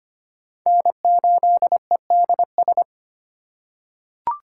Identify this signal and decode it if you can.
Morse code, heard as a keyed tone, about 25 wpm, 700 Hz. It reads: N8EDS
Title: USA Call Signs - 4x Character Spacing - 25wpm